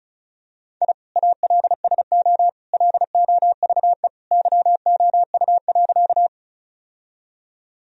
Morse code, heard as a keyed tone, about 36 wpm, 700 Hz. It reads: I ALSO LOVE YOU.
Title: Sentences from Top 1000 Words - Review Part 1 - 35wpm